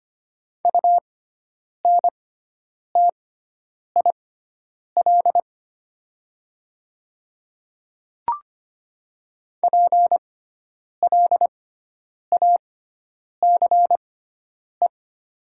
Morse code, heard as a keyed tone, about 25 wpm, spaced as for 9 wpm, 700 Hz. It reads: UNTIL PLACE
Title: 6x Character Spacing - Top 500 Words - 25wpm